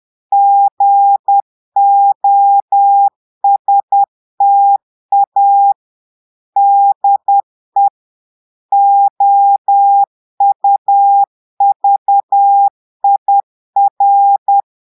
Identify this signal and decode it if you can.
Morse code, heard as a keyed tone, about 10 wpm, 800 Hz. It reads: GOSTA DE OUVIR
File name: portugais texte en clair 2 vitesse 10 mots minutes